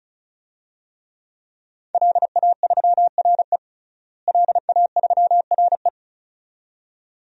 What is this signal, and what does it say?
Morse code, heard as a keyed tone, about 36 wpm, 700 Hz. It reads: LA3RE LA3RE
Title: DXCC-100 Call Signs - 5 Characters - Repeated 2x - 35wpm